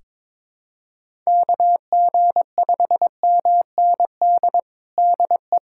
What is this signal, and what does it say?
Morse code, heard as a keyed tone, about 22 wpm, 700 Hz. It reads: KG5MND DE
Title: USA Callsign DE Callsign 22wpm